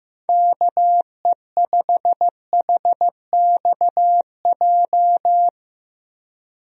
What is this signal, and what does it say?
Morse code, heard as a keyed tone, about 16 wpm, 700 Hz. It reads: KE5HXJ
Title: USA Call Signs - 6 Characters - Repeated 2x - 15wpm